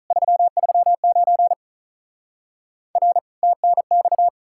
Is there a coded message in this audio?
Yes: Morse code from 0.1 to 4.3 s, heard as a keyed tone, about 48 wpm, 700 Hz. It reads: 339 R TNX